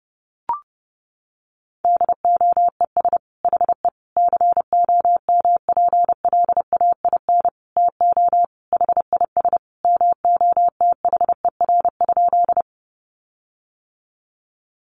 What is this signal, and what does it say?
Morse code, heard as a keyed tone, about 30 wpm, 700 Hz. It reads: DOES HE COMPLAIN TO HIS MOTHER?
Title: Sentences from Top 1300 Words 30wpm